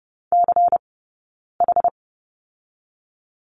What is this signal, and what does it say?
Morse code, heard as a keyed tone, about 30 wpm, 700 Hz. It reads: C H
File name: Lesson 17d - Letter u - Two Characters 30wpm